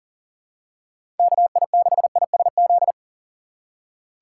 Morse code, heard as a keyed tone, about 40 wpm, 700 Hz. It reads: KI6ISZ
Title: USA Call Signs - 3x Character Spacing - 40wpm